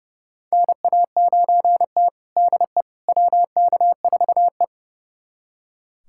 Morse code, heard as a keyed tone, about 31 wpm, 700 Hz. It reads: NA9T DE WK4E
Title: USA Callsign DE Callsign - 4 Characters - 30wpm